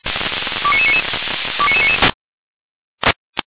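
Five-tone selective call (ZVEI), heard as quick runs of five tones, tones of about 70 ms. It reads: 29999 29999
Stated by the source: ZVEI five-tone selective call, code 29999, received via SDR.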